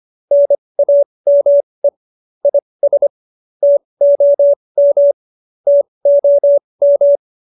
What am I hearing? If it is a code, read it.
Morse code, heard as a keyed tone, about 25 wpm, spaced as for 20 wpm, 570 Hz. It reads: NAME IS TOM TOM